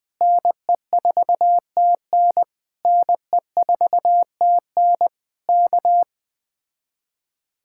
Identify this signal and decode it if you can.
Morse code, heard as a keyed tone, about 20 wpm, 700 Hz. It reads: NE4TN NE4TN K
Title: Speed-Racing Common QSO Phrases - 17wpm